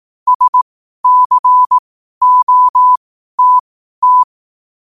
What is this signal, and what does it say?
Morse code, heard as a keyed tone, about 18 wpm, spaced as for 12 wpm, 1000 Hz. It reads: SCOTT
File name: qotdfiles / 12wpm